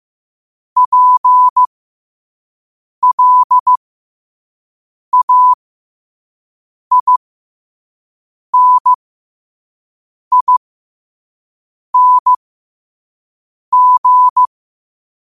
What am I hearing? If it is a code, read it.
Morse code, heard as a keyed tone, about 15 wpm, spaced as for 5 wpm, 1000 Hz. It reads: PLAINING
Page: New quotes every day in morse code at 5 Words per minute.